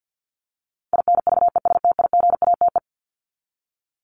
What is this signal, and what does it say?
Morse code, heard as a keyed tone, about 48 wpm, 700 Hz. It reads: INVESTIGATE
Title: Top 1751-2000 Words 50wpm